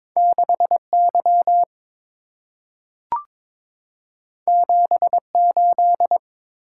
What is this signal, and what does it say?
Morse code, heard as a keyed tone, about 22 wpm, 690 Hz. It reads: 6Y 78
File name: practice / 22wpm / Two Letter-Number Combinations - 22wpm